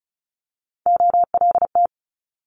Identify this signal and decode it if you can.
Morse code, heard as a keyed tone, about 36 wpm, 700 Hz. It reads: OLT
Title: Multi-letter Phonemes Quick Fire 35wpm